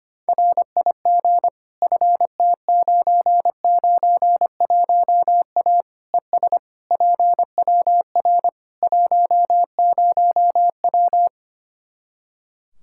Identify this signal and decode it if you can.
Morse code, heard as a keyed tone, about 25 wpm, 700 Hz. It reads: RIG FT991A ES PWR 10W
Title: Practice QSO - Exchange 2 - Info - 25wpm